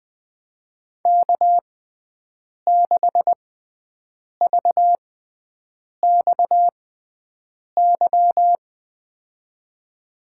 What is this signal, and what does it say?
Morse code, heard as a keyed tone, about 20 wpm, spaced as for 7 wpm, 700 Hz. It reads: K6VXY
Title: USA Call Signs - 6x Character Spacing - 20wpm